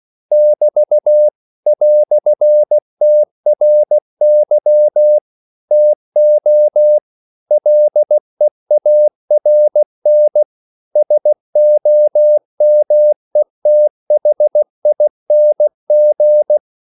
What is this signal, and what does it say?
Morse code, heard as a keyed tone, about 16 wpm, 600 Hz. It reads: <BT> "TRY TO LEARN SOMETHING
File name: CW / Practice / EnglishQuotations0116